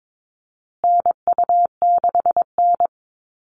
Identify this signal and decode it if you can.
Morse code, heard as a keyed tone, about 22 wpm, 700 Hz. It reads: NU6N
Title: Call Signs Repeated 3x - 22wpm